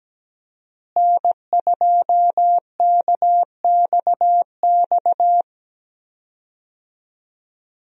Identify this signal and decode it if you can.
Morse code, heard as a keyed tone, about 17 wpm, 700 Hz. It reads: N2KXX
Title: USA Call Signs - 5 Characters - Repeated 3x - 17wpm